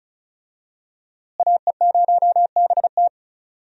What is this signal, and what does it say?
Morse code, heard as a keyed tone, about 36 wpm, 700 Hz. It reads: AE0BT
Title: USA Call Signs - 5 Characters - Repeated 2x - 35wpm